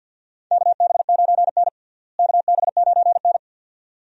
Morse code, heard as a keyed tone, about 51 wpm, 700 Hz. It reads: KB9N KB9N
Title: USA Call Signs - 4 Characters - Repeated 2x - 50wpm